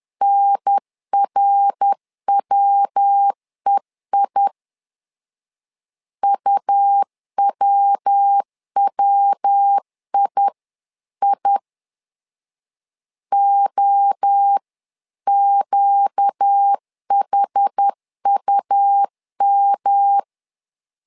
Morse code, heard as a keyed tone, about 10 wpm, 800 Hz. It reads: NRWEI UWWII OQHUM
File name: LEZIONI / 4 Q 10wpm 2210